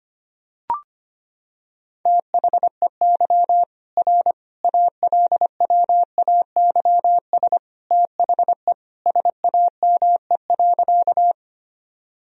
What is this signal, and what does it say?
Morse code, heard as a keyed tone, about 25 wpm, 700 Hz. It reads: THEY R ALWAYS THE SAME.
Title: Sentences from Top 300 Words and Abbreviations 25wpm